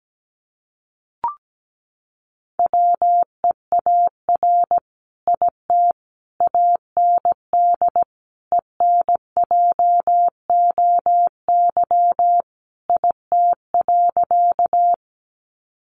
Morse code, heard as a keyed tone, about 17 wpm, 700 Hz. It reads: WEAR IT AND ENJOY IT.
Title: Sentences from Top 1500 Words 17wpm